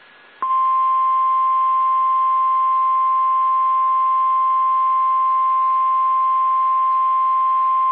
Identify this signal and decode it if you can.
Emergency Alert System (SAME) tones (the 1050 Hz attention tone). It is heard from 1.2 to 7.3 s.
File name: eas / KSDZKDJL / KSDZKDJL